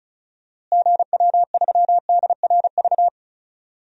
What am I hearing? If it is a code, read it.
Morse code, heard as a keyed tone, about 34 wpm, 700 Hz. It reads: GW3DRV
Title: DXCC-100 Call Signs - 6 Characters - Repeated 3x - 35wpm